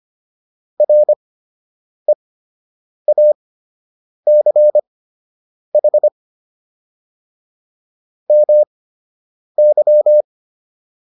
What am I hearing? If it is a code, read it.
Morse code, heard as a keyed tone, about 25 wpm, spaced as for 8 wpm, 600 Hz. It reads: REACH MY